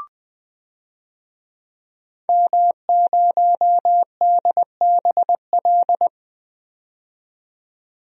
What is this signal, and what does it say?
Morse code, heard as a keyed tone, about 21 wpm, 700 Hz. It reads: M0DBL
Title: DXCC-100 Call Signs - 5 Characters - 20wpm